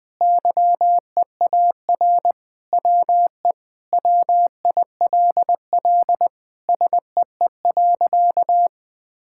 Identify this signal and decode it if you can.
Morse code, heard as a keyed tone, about 20 wpm, 700 Hz. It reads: YEAR WE WILL SEE.
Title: Sentences with 3-Word Phrases - Spoken Beginning - 20wpm